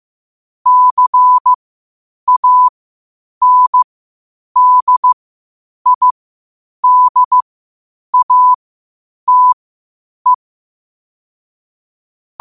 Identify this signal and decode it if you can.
Morse code, heard as a keyed tone, about 15 wpm, spaced as for 8 wpm, 1000 Hz. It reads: CANDIDATE